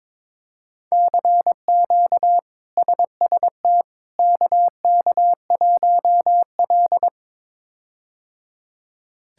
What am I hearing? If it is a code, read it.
Morse code, heard as a keyed tone, about 21 wpm, 700 Hz. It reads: CQ SST KK1L
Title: K1USN SST QSO Practice - 22wpm